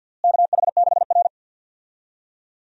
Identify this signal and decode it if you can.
Morse code, heard as a keyed tone, about 52 wpm, 700 Hz. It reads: KH6R
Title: USA Call Signs - 4 Characters - Repeated 3x - 50wpm